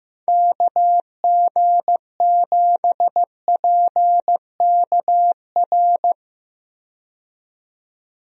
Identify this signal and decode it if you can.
Morse code, heard as a keyed tone, about 15 wpm, 700 Hz. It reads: KG7PKR
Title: USA Call Signs - 6 Characters - 15wpm